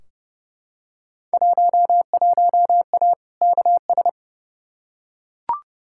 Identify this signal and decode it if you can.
Morse code, heard as a keyed tone, about 32 wpm, 700 Hz. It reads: 11A KS
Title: Field Day Exchanges Encore - 30wpm